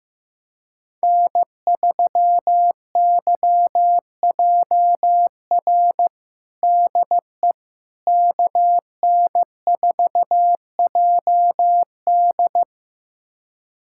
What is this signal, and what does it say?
Morse code, heard as a keyed tone, about 16 wpm, 700 Hz. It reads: N3YJR DE KN4JD
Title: USA Callsign DE Callsign - 5 Characters - 15wpm